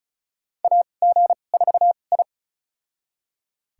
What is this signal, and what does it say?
Morse code, heard as a keyed tone, about 34 wpm, spaced as for 25 wpm, 700 Hz. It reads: AG4I
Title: USA Call Signs - 2x Character Spacing - 35wpm